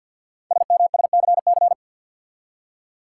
Morse code, heard as a keyed tone, about 50 wpm, 690 Hz. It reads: SMSCC
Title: Code Groups - 5 Letters - 50wpm